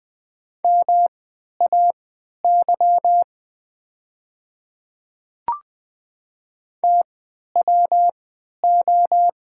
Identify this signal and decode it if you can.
Morse code, heard as a keyed tone, about 20 wpm, spaced as for 11 wpm, 700 Hz. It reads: MAY TWO
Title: 3x Character Spacing - Top 100 Words - 20wpm